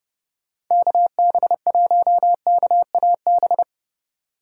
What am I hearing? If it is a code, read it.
Morse code, heard as a keyed tone, about 30 wpm, 700 Hz. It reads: KB1KAB
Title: USA Call Signs - 6 Characters - 3x Character Spacing - 30wpm